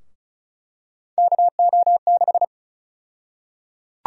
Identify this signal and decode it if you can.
Morse code, heard as a keyed tone, about 34 wpm, 700 Hz. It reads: KO6
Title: USA Call Sign Prefixes - 35wpm